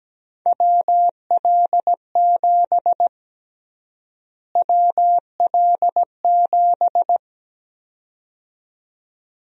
Morse code, heard as a keyed tone, about 17 wpm, 700 Hz. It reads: WL7 WL7
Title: USA Call Sign Prefixes - Repeated 2x - 17wpm